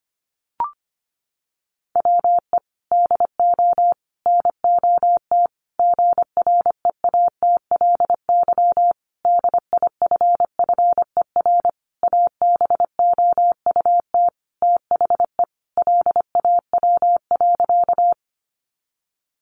Morse code, heard as a keyed tone, about 25 wpm, 700 Hz. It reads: WE DO NOT GREATLY DIFFER ABOUT THE LAW.
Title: Sentences from Top 1400 Words 25wpm